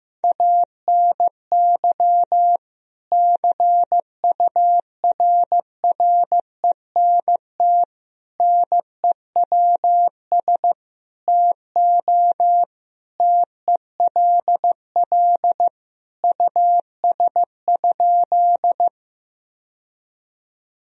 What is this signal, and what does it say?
Morse code, heard as a keyed tone, about 16 wpm, 700 Hz. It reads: ANY CURRENT NEWS TO TELL US?
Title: Sentences from Top 1000 Words - Review Part 4 - 15wpm